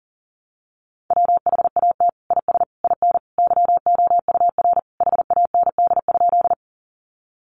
Morse code, heard as a keyed tone, about 40 wpm, 700 Hz. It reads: WHAT IS IN YOUR HAND?
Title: Sentences from Top 200 Words 40wpm